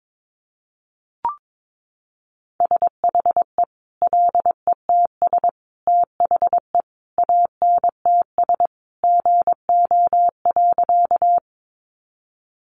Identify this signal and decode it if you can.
Morse code, heard as a keyed tone, about 22 wpm, 700 Hz. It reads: SHE LETS THE ANTS GO.
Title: Additional Easy Phrases 22wpm